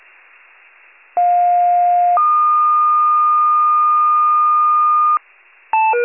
Two-tone page, heard as a one-second tone, then a three-second tone, about 710 then 1220 Hz. It is heard from 1.2 to 5.2 s.
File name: wcalert